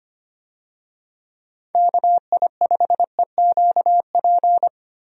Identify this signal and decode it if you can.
Morse code, heard as a keyed tone, about 25 wpm, 700 Hz. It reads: KI5EQP
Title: USA Call Signs - 6 Characters - Repeated 3x - 25wpm